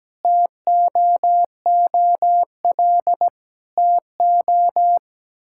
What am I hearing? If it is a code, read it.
Morse code, heard as a keyed tone, about 17 wpm, 700 Hz. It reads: TOOL TO
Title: Sentences from Top 1000 Words - Review Part 6 - 17wpm